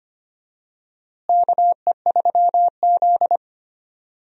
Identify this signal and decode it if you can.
Morse code, heard as a keyed tone, about 25 wpm, 700 Hz. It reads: KE3Z
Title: USA Call Signs - 4 Characters - Repeated 3x - 25wpm